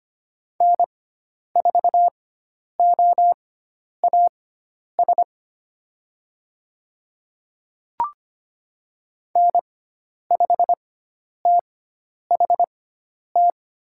Morse code, heard as a keyed tone, about 25 wpm, spaced as for 10 wpm, 700 Hz. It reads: N4OAS N5THT
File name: Lesson 16g - Letter c - Call Signs 25wpm